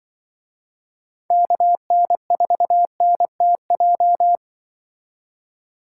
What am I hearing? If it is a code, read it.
Morse code, heard as a keyed tone, about 25 wpm, 700 Hz. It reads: KN4NTJ
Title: Speed Racing - USA 2x3 Call Signs 20wpm